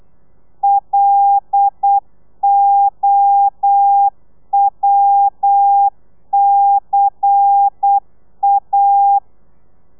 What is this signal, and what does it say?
Morse code, heard as a keyed tone, about 8 wpm, 800 Hz. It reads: LOWCA
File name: telegrafia / i2tfj / Lez10 35